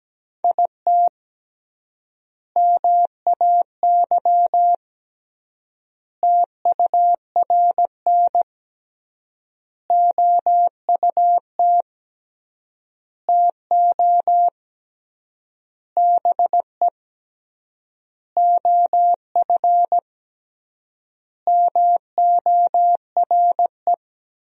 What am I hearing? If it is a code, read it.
Morse code, heard as a keyed tone, about 17 wpm, 700 Hz. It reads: IT MAY TURN OUT TO BE OF MORE
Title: 3x Word-Spacing - Sentences from Top 400 Words - 17wpm